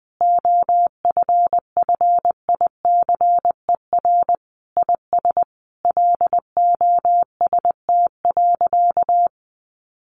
Morse code, heard as a keyed tone, about 20 wpm, 700 Hz. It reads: OFFICER IS LOST.
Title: Sentences from Top 1400 Words 20wpm